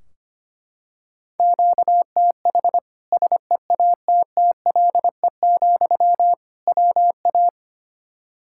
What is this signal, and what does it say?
Morse code, heard as a keyed tone, about 25 wpm, 700 Hz. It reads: QTH SEATTLE, WA
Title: Practice QSO - Exchange 1 - Beginning - 25wpm